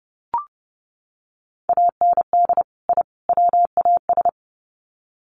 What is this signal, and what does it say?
Morse code, heard as a keyed tone, about 30 wpm, 700 Hz. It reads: AND I WAS
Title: Sets of 3 Words 30wpm